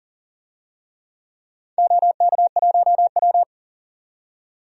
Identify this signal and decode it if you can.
Morse code, heard as a keyed tone, about 38 wpm, 700 Hz. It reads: OK1W
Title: DXCC-100 Call Signs - 4 Characters - 40wpm